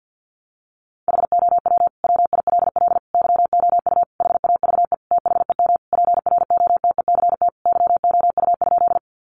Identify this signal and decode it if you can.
Morse code, heard as a keyed tone, about 48 wpm, 700 Hz. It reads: HOW WILL YOU HAVE THEM PROTECT YOU?
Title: Sentences from Top 900 Words 50wpm